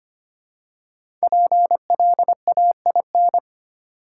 Morse code, heard as a keyed tone, about 25 wpm, 700 Hz. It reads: PLAIN
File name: 5 Letter Words - First 3 Letters Given - 25wpm